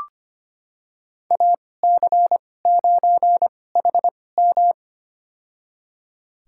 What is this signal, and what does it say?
Morse code, heard as a keyed tone, about 25 wpm, spaced as for 18 wpm, 700 Hz. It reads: AC9HM
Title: USA Call Signs - 2x Character Spacing - 25wpm